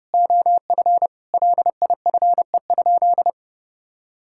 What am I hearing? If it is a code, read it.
Morse code, heard as a keyed tone, about 32 wpm, 700 Hz. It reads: OF LIFE?
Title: Sentences from Top 1000 Words - Review Part 5 - 30wpm